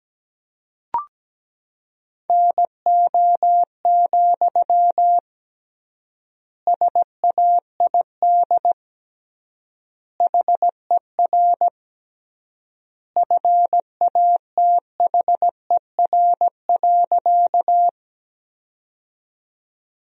Morse code, heard as a keyed tone, about 17 wpm, 700 Hz. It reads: NO, SAID HER FATHER.
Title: 3x Word-Spacing - Sentences from Top 300 Words - 17wpm